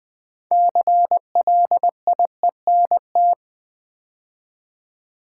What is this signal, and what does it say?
Morse code, heard as a keyed tone, about 20 wpm, 700 Hz. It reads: CLIENT
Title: Build A Word - 3 Letters At A Time - 6-Letter Words - 20wpm